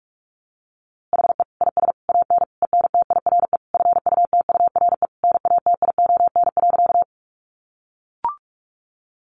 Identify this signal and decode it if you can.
Morse code, heard as a keyed tone, about 44 wpm, 700 Hz. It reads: HE IS AN ENTIRE FUTURE NATION.
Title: A Sound Of Thunder 45wpm